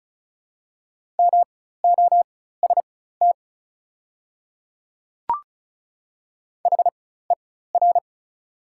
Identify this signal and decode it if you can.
Morse code, heard as a keyed tone, about 34 wpm, spaced as for 16 wpm, 690 Hz. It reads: MOST HER
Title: 4x Character Spacing - Top 100 Words - 35wpm